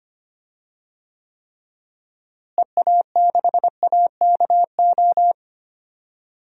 Morse code, heard as a keyed tone, about 25 wpm, 700 Hz. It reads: EA6AKO
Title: DXCC-100 Call Signs - 6 Characters - Repeated 3x - 25wpm